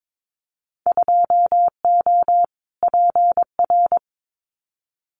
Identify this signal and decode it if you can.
Morse code, heard as a keyed tone, about 22 wpm, 700 Hz. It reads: 2O PR
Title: Winter Field Day Exchanges Encore 22wpm